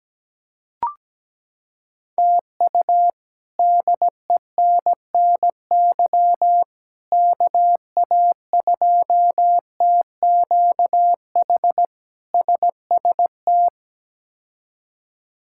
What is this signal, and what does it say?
Morse code, heard as a keyed tone, about 17 wpm, 700 Hz. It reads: TU DENNY KA2TQH SST
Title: K1USN SST QSO Practice - 17wpm